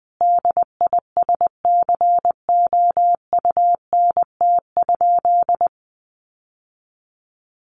Morse code, heard as a keyed tone, about 19 wpm, 700 Hz. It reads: DISCOUNT?
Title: Sentences from Top 1200 Words 20wpm